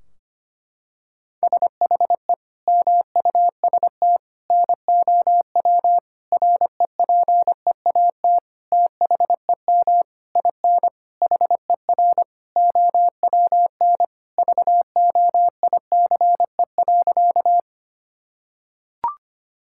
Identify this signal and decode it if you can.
Morse code, heard as a keyed tone, about 25 wpm, 700 Hz. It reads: SHE MUST NOW REPEAT THEM IN HER OWN VOICE.
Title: Sentences from Top 1600 Words 25wpm